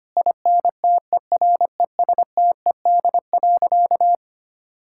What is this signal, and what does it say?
Morse code, heard as a keyed tone, about 25 wpm, 700 Hz. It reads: INTERESTED.
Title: Sentences from Top 1000 Words - Review Part 6 - 25wpm